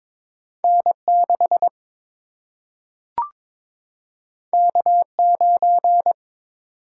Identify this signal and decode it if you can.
Morse code, heard as a keyed tone, about 22 wpm, 700 Hz. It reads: N6 K9
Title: Single-Letter Single-Number 22wpm